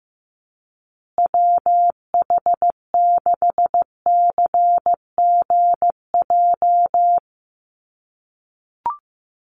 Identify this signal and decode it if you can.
Morse code, heard as a keyed tone, about 15 wpm, 700 Hz. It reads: WH6CGJ
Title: USA Call Signs Repeated 3x 15wpm